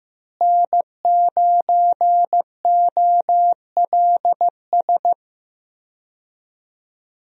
Morse code, heard as a keyed tone, about 15 wpm, 700 Hz. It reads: N9OLS
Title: USA Call Signs - 5 Characters - Repeated 2x - 15wpm